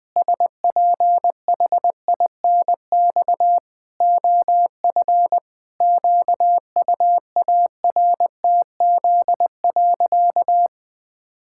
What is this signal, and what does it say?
Morse code, heard as a keyed tone, about 20 wpm, 700 Hz. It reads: SPHINX OF QUARTZ.
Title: Pangram Challenge v2 - 20wpm